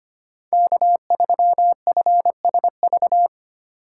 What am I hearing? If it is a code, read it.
Morse code, heard as a keyed tone, about 25 wpm, 700 Hz. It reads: K3FSV
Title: USA Call Signs - 5 Characters - 3x Character Spacing - 25wpm